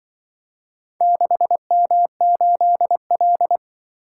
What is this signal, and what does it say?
Morse code, heard as a keyed tone, about 24 wpm, 700 Hz. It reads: 6M8L
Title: Speed-Racing - Four Letter-Number Combinations - 20wpm